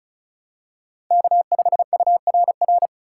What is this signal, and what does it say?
Morse code, heard as a keyed tone, about 34 wpm, 700 Hz. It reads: K5URR
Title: USA Call Signs - 5 Characters - 3x Character Spacing - 35wpm